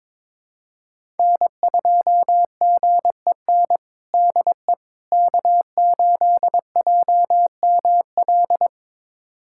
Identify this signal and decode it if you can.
Morse code, heard as a keyed tone, about 22 wpm, 700 Hz. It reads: N2GEN DE K8JML
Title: USA Callsign DE Callsign - 5 Characters - 22wpm